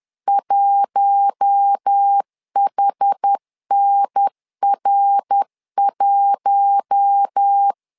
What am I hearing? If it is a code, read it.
Morse code, heard as a keyed tone, about 10 wpm, 800 Hz. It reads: 1HNR1
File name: LEZIONI / 4 Q1 10wpm 2448